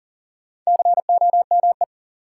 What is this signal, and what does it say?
Morse code, heard as a keyed tone, about 37 wpm, 700 Hz. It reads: COME
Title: Sets of 2 Words - New Word Drill - 40wpm